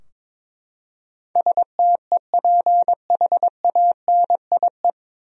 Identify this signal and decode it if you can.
Morse code, heard as a keyed tone, about 23 wpm, 700 Hz. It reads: STEPHANIE
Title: USA Common Names - 22wpm